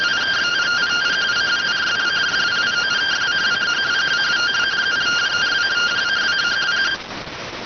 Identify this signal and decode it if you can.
Two-tone teleprinter signal, about 170 Hz shift.